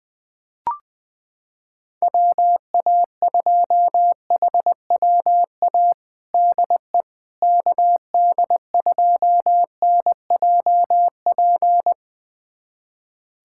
Morse code, heard as a keyed tone, about 21 wpm, 700 Hz. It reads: WA2HWA DE KD2NJP
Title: USA Callsign DE Callsign - 6 Characters - 20wpm